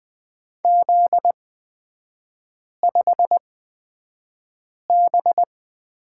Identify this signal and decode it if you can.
Morse code, heard as a keyed tone, about 20 wpm, spaced as for 5 wpm, 700 Hz. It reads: Z5B
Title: 243 Lesson 39e - Letter z - Three Characters 20wpm